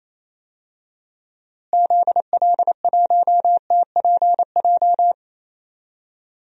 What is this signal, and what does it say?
Morse code, heard as a keyed tone, about 28 wpm, 700 Hz. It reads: ZL1TPJ
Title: DXCC-100 Call Signs - 6 Characters - Repeated 3x - 28wpm